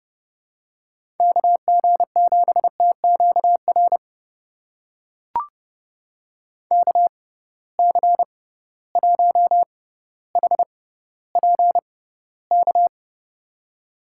Morse code, heard as a keyed tone, about 30 wpm, spaced as for 15 wpm, 700 Hz. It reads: KG7TQR KC1HPK
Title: USA Call Signs - 6 Characters - 6x Character Spacing - 30wpm